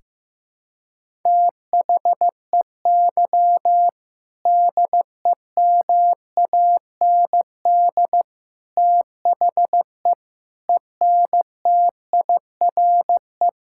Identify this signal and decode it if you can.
Morse code, heard as a keyed tone, about 16 wpm, 700 Hz. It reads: THEY DEMAND THE ENTIRE
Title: Sentences from Top 1000 Words - Review Part 5 - 15wpm